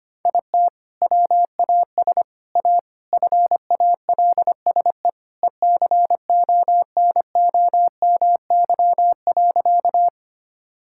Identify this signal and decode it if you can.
Morse code, heard as a keyed tone, about 25 wpm, 700 Hz. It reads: IT WAS A FALSE ECONOMY.
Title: Sentences from Top 1600 Words 25wpm